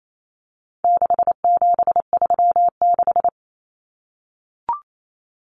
Morse code, heard as a keyed tone, about 28 wpm, 700 Hz. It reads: 6736
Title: Common Cut Numbers - 4-Digit 28wpm